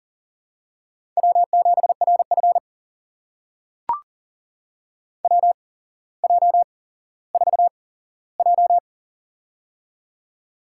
Morse code, heard as a keyed tone, about 38 wpm, spaced as for 17 wpm, 700 Hz. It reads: W7RF WJ4J
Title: USA Call Signs - 4 Characters - 8x Character Spacing - 40wpm